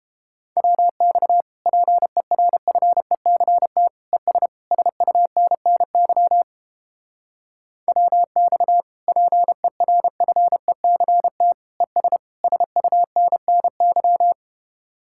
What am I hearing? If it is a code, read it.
Morse code, heard as a keyed tone, about 29 wpm, 700 Hz. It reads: WX PERFECT ES SUNNY WX PERFECT ES SUNNY